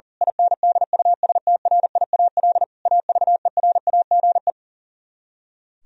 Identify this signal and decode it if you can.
Morse code, heard as a keyed tone, about 38 wpm, 700 Hz. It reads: INDUSTRIAL AVERAGE
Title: Speed-Racing - Sets of 5 Words 40wpm